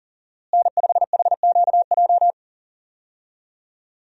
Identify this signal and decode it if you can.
Morse code, heard as a keyed tone, about 40 wpm, 700 Hz. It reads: N5HQJ
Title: USA Call Signs - 5 Characters - Repeated 3x - 40wpm